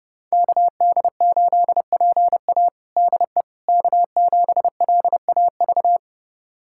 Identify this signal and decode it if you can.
Morse code, heard as a keyed tone, about 30 wpm, 700 Hz. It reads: KD8PA DE K7LAV
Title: USA Callsign DE Callsign - 5 Characters - 30wpm